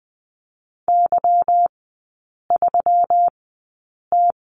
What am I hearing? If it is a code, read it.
Morse code, heard as a keyed tone, about 20 wpm, spaced as for 8 wpm, 690 Hz. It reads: Y3T
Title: 145 Lesson 24e - Letter y - Three Characters 20wpm